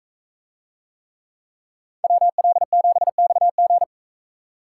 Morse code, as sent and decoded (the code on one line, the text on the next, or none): .-- .-.. --... -..- --.
WL7XG